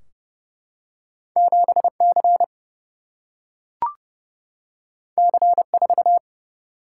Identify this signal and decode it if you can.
Morse code, heard as a keyed tone, about 30 wpm, 700 Hz. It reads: 7 C C 4
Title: Single-Letter Single-Number 30wpm